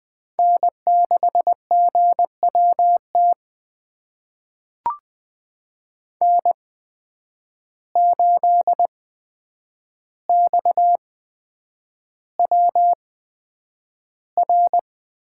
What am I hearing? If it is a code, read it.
Morse code, heard as a keyed tone, about 20 wpm, spaced as for 9 wpm, 700 Hz. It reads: N6GWT N8XWR
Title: USA Call Signs - 5 Characters - 8x Character Spacing - 20wpm